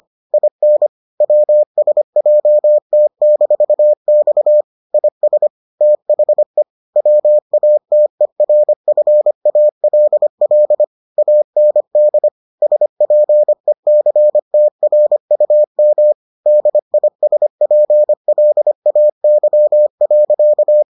Morse code, heard as a keyed tone, about 25 wpm, 600 Hz. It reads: IN WSJT-X IS THE WATERFALL AND SPECTRUM DISPLAY.